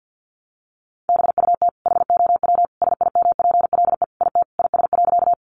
Morse code, heard as a keyed tone, about 50 wpm, 700 Hz. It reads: BUT HOW SIMPLE IT IS.
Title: Sentences from Top 400 Words 50wpm